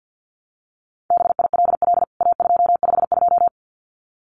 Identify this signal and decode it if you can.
Morse code, heard as a keyed tone, about 50 wpm, 700 Hz. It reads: BILL A252
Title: CWT Exchange 50wpm